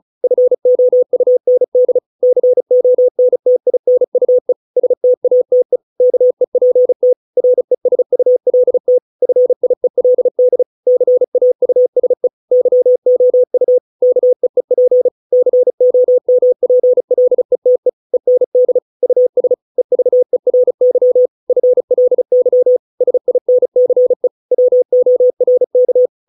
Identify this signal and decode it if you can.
Morse code, heard as a keyed tone, about 34 wpm, 500 Hz. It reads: FOUND CONTINUE STATE KEPT RESULT FIELD CAUSE YOU KEEP COMPLETE END US EVERY FLY SINCE WORK